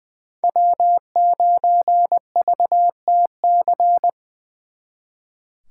Morse code, heard as a keyed tone, about 24 wpm, 700 Hz. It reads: W9VTC
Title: Speed Racing - USA 1x3 Call Signs 20wpm